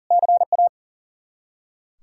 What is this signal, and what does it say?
Morse code, heard as a keyed tone, about 37 wpm, 700 Hz. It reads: CA
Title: Practice QSO - Exchange 1 - Beginning - 40wpm